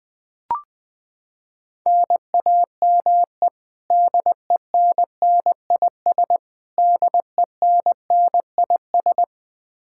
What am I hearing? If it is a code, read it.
Morse code, heard as a keyed tone, about 21 wpm, 700 Hz. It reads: NAME DENNIS DENNIS
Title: Practice QSO - Exchange 1 - Beginning - 20wpm